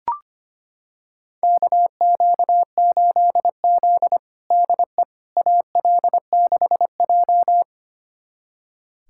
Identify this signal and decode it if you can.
Morse code, heard as a keyed tone, about 25 wpm, 700 Hz. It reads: KQ8Z DE AL6J
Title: USA Callsign DE Callsign - 4 Characters - 25wpm